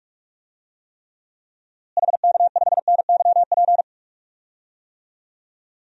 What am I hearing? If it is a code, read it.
Morse code, heard as a keyed tone, about 41 wpm, 700 Hz. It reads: HK5NYP
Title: DXCC-100 Call Signs - 6 Characters - Repeated 2x - 45wpm